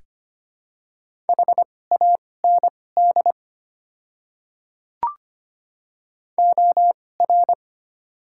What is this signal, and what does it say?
Morse code, heard as a keyed tone, about 25 wpm, spaced as for 18 wpm, 700 Hz. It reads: HAND OR
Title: 2x Character Spacing - Top 200 Words - 25wpm